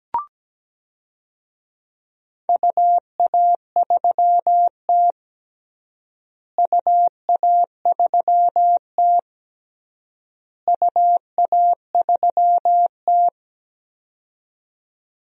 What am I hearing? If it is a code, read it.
Morse code, heard as a keyed tone, about 17 wpm, 700 Hz. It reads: UA3T UA3T UA3T
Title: DXCC-100 Call Signs - 4 Characters - Repeated 3x - 17wpm